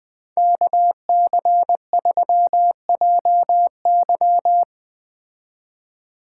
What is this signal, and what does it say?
Morse code, heard as a keyed tone, about 20 wpm, 700 Hz. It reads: KC3JY
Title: Speed Racing - USA 2x2 Call Signs 17wpm